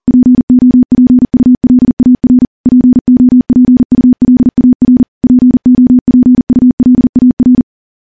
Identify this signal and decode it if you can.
Morse code, heard as a keyed tone, about 40 wpm, 250 Hz. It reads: POPULAR POPULAR POPULAR
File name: morse / morse-training-40WPM